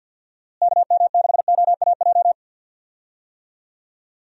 Morse code, heard as a keyed tone, about 50 wpm, 700 Hz. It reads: KM6OAJ
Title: USA Call Signs - 6 Characters - Repeated 3x - 50wpm